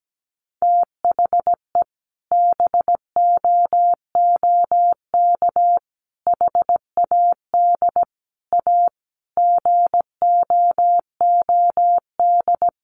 Morse code, heard as a keyed tone, about 17 wpm, 700 Hz. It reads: THE BOOK HAD A GOOD
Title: Sentences from Top 1400 Words 17wpm